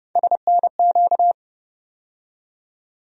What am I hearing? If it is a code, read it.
Morse code, heard as a keyed tone, about 30 wpm, 700 Hz. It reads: SNQ
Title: USA Call Sign Postfixes - 3 Letters 30wpm